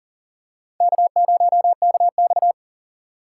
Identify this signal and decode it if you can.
Morse code, heard as a keyed tone, about 38 wpm, 700 Hz. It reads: K0KX
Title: K1USN SST QSO Practice - 40wpm